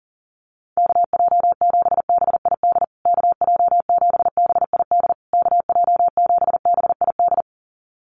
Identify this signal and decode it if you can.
Morse code, heard as a keyed tone, about 40 wpm, 700 Hz. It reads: KJ7BID KJ7BID KJ7BID
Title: USA Call Signs Repeated 3x 40wpm